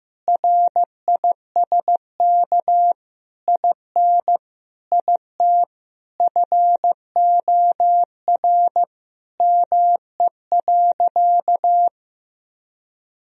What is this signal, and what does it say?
Morse code, heard as a keyed tone, about 15 wpm, 700 Hz. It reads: RISK IN IT FOR ME.
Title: Sentences from Top 1000 Words - Review Part 4 - 15wpm